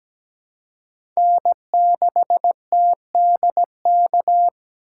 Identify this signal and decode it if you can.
Morse code, heard as a keyed tone, about 17 wpm, 700 Hz. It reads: N6TDK
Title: USA Call Signs - 5 Characters - 4x Character Spacing - 17wpm